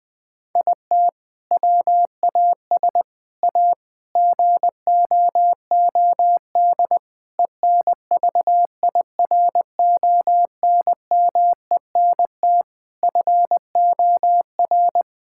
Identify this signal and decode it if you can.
Morse code, heard as a keyed tone, about 20 wpm, 700 Hz. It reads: IT WAS A GOOD ENVIRONMENT FOR
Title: Sentences from Top 1000 Words - Review Part 5 - 20wpm